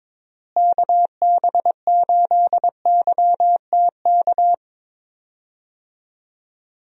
Morse code, heard as a keyed tone, about 22 wpm, 700 Hz. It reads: KB8YTK
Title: USA Call Signs - 6 Characters - Repeated 2x - 22wpm